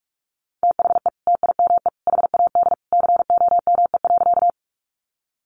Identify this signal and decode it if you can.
Morse code, heard as a keyed tone, about 44 wpm, 700 Hz. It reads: THE TIME HAD COME.
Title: Sentences from Top 100 Words 45wpm